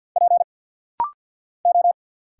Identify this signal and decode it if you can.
Morse code, heard as a keyed tone, about 50 wpm, 690 Hz. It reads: P O
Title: Single Letters - Rapid-Fire 50wpm